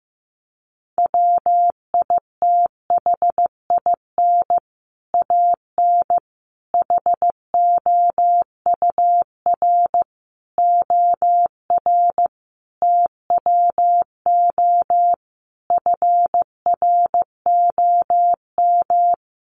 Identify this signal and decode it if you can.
Morse code, heard as a keyed tone, about 15 wpm, 700 Hz. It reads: WITHIN AN HOUR OR TWO FROM
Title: Sentences from Top 700 Words 15wpm